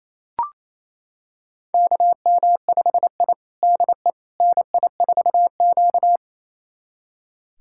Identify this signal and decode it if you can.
Morse code, heard as a keyed tone, about 27 wpm, 700 Hz. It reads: KM5I DE NI4Q
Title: USA Callsign DE Callsign - 4 Characters - 28wpm